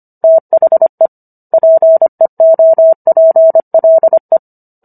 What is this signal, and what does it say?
Morse code, heard as a keyed tone, about 25 wpm, 650 Hz. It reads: THE PEOPLE